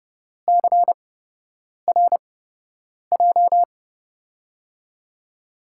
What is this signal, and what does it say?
Morse code, heard as a keyed tone, about 30 wpm, spaced as for 8 wpm, 700 Hz. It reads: CRJ
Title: USA Call Signs - 5 Characters - 8x Character Spacing - 30wpm